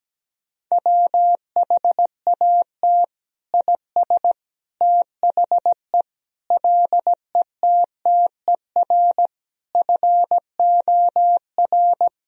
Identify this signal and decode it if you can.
Morse code, heard as a keyed tone, about 17 wpm, 700 Hz. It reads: WHAT IS THE LETTER FOR
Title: Sentences from Top 1000 Words - Review Part 2 - 17wpm